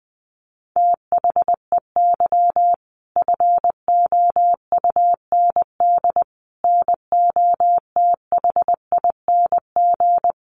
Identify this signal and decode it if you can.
Morse code, heard as a keyed tone, about 20 wpm, 700 Hz. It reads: THEY FOUND NOTHING
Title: Sentences from Top 1500 Words 20wpm